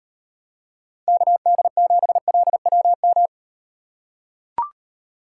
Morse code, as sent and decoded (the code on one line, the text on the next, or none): -.- -.. --... .-.. .-- --
KD7LWM